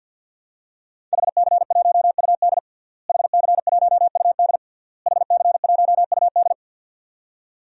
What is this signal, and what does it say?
Morse code, heard as a keyed tone, about 52 wpm, 700 Hz. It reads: HC1UD HC1UD HC1UD
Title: DXCC-100 Call Signs - 5 Characters - Repeated 3x - 50wpm